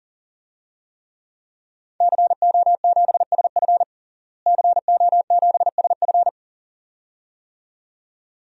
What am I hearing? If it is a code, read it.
Morse code, heard as a keyed tone, about 40 wpm, 700 Hz. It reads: CO7SF CO7SF
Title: DXCC-100 Call Signs - 5 Characters - Repeated 2x - 40wpm